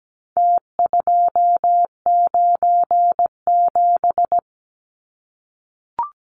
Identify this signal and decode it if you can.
Morse code, heard as a keyed tone, about 17 wpm, 700 Hz. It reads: T297
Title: Numbers - Contest Style - 4-Digit 17wpm